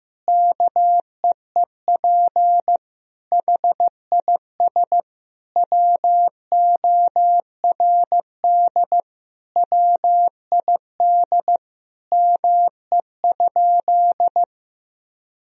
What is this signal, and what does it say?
Morse code, heard as a keyed tone, about 16 wpm, 700 Hz. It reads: KEEP HIS WORD WID ME?
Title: Sentences from Top 300 Words and Abbreviations 15wpm